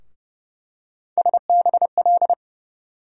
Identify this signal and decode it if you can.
Morse code, heard as a keyed tone, about 30 wpm, 700 Hz. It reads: SBL
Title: USA Call Sign Postfixes - 3 Letters 30wpm